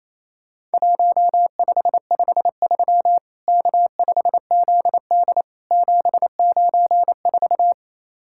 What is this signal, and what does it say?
Morse code, heard as a keyed tone, about 28 wpm, 700 Hz. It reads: 1553 K5ZD 794
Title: Worked All Europe DX Contest - QTC Exchange - 28wpm